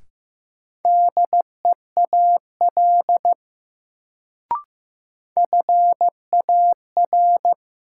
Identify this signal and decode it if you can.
Morse code, heard as a keyed tone, about 15 wpm, 700 Hz. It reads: DEAL FAR
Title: Sets of 3 Words - New Word Drill - 15wpm